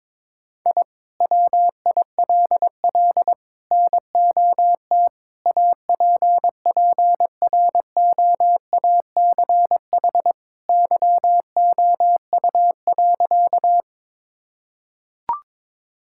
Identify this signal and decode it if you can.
Morse code, heard as a keyed tone, about 22 wpm, 700 Hz. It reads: I WILL NOT APPROACH YOU.
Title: Sentences from Top 1000 Words - Review Part 6 - 22wpm